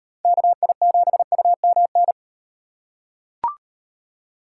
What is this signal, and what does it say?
Morse code, heard as a keyed tone, about 38 wpm, 700 Hz. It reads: KI7UMN